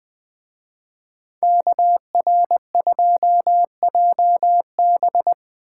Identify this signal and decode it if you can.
Morse code, heard as a keyed tone, about 20 wpm, 700 Hz. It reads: KR2JB